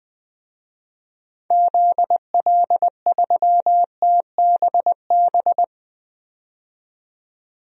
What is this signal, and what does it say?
Morse code, heard as a keyed tone, about 20 wpm, 700 Hz. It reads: ZL3TBB
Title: DXCC-100 Call Signs - 6 Characters - 20wpm